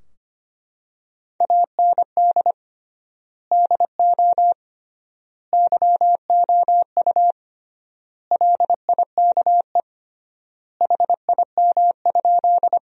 Morse code, heard as a keyed tone, about 25 wpm, 700 Hz. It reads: AND DO YOU LIKE HIM?
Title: 3x Word-Spacing - Sentences from Top 100 Words - 25wpm